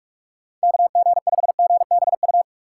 Morse code, heard as a keyed tone, about 46 wpm, 700 Hz. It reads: KG5GBU
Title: USA Call Signs - 6 Characters - 2x Character Spacing - 45wpm